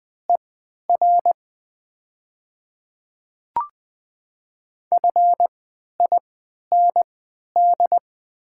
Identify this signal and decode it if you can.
Morse code, heard as a keyed tone, about 20 wpm, spaced as for 11 wpm, 700 Hz. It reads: ER FIND
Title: 3x Character Spacing - Top 100 Words - 20wpm